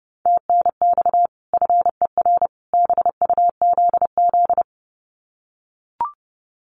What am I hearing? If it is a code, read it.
Morse code, heard as a keyed tone, about 30 wpm, 700 Hz. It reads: TNX FER BUZZ
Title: Common QSO Phrases 30wpm